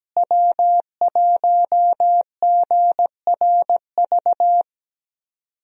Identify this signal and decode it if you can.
Morse code, heard as a keyed tone, about 17 wpm, 700 Hz. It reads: W1GRV
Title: USA Call Signs - 5 Characters - Repeated 2x - 17wpm